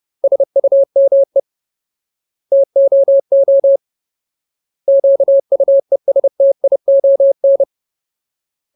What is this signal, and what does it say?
Morse code, heard as a keyed tone, about 30 wpm, 550 Hz. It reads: SUME TOO QUESTION
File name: Ultimate Word List / 30WPM Ultimate